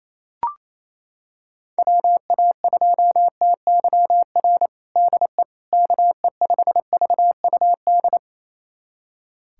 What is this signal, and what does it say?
Morse code, heard as a keyed tone, about 27 wpm, 700 Hz. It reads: WA2TYR DE KE5VUD
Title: USA Callsign DE Callsign - 6 Characters - 28wpm